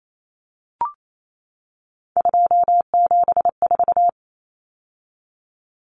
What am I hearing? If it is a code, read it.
Morse code, heard as a keyed tone, about 28 wpm, 700 Hz. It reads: 274
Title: Numbers - 3-Digit 28wpm